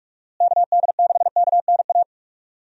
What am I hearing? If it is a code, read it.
Morse code, heard as a keyed tone, about 46 wpm, 700 Hz. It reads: KD6KNA
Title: USA Call Signs - 6 Characters - Repeated 2x - 45wpm